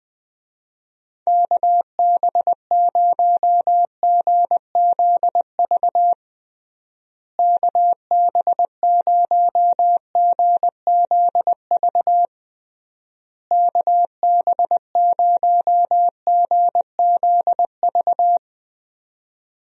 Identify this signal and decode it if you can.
Morse code, heard as a keyed tone, about 21 wpm, 700 Hz. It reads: KB0GZV KB0GZV KB0GZV
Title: USA Call Signs - 6 Characters - Repeated 3x - 20wpm